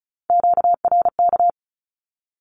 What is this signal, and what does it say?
Morse code, heard as a keyed tone, about 34 wpm, 700 Hz. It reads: QRK
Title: Q-Codes Repeated 3x 35wpm